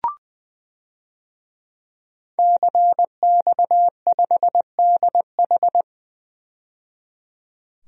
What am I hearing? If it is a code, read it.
Morse code, heard as a keyed tone, about 20 wpm, 700 Hz. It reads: CX5DH
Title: DXCC-100 Call Signs - 5 Characters - 20wpm